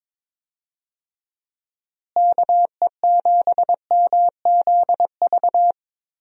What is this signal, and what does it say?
Morse code, heard as a keyed tone, about 22 wpm, 700 Hz. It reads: KE7MZV
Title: USA Call Signs - 6 Characters - Repeated 2x - 22wpm